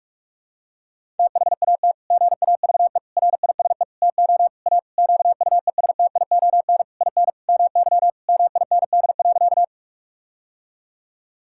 Morse code, heard as a keyed tone, about 46 wpm, 700 Hz. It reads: THAT GAVE RISE TO A QUESTION IN MY MIND.
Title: Sentences from Top 1000 Words - Review Part 6 - 45wpm